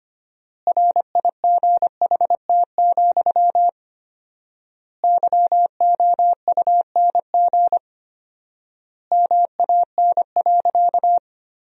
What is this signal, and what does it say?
Morse code, heard as a keyed tone, about 25 wpm, 700 Hz. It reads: RIGHT, YOUNG MAN.
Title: 4x Word-Spacing - Sentences from Top 400 Words - 25wpm